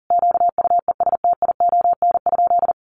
Morse code, heard as a keyed tone, about 40 wpm, 700 Hz. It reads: QUESTION?
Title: Sentences from Top 700 Words 40wpm